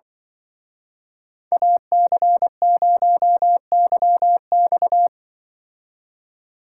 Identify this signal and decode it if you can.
Morse code, heard as a keyed tone, about 24 wpm, 700 Hz. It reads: AC0YX
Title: Speed Racing - USA 2x2 Call Signs 20wpm